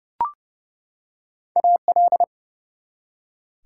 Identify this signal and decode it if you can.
Morse code, heard as a keyed tone, about 29 wpm, 700 Hz. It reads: AL
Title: Prefixes and Suffixes 30wpm